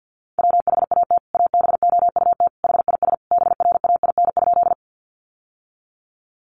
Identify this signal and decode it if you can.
Morse code, heard as a keyed tone, about 50 wpm, 700 Hz. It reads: WHAT ABOUT HIS BRAIN?
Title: Sentences from Top 1100 Words 50wpm